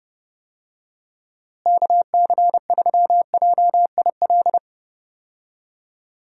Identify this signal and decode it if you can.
Morse code, heard as a keyed tone, about 30 wpm, 700 Hz. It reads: KC3JIL
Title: USA Call Signs - 6 Characters - Repeated 3x - 30wpm